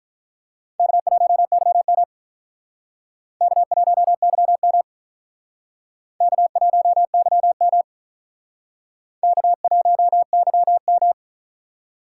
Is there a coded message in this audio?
Yes: Morse code from 0.8 to 11.1 s, heard as a keyed tone, about 44 wpm, 700 Hz. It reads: K1YM K1YM K1YM K1YM